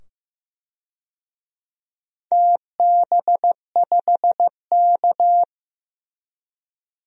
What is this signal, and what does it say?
Morse code, heard as a keyed tone, about 15 wpm, 700 Hz. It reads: TB5K
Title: DXCC-100 Call Signs - 4 Characters - 15wpm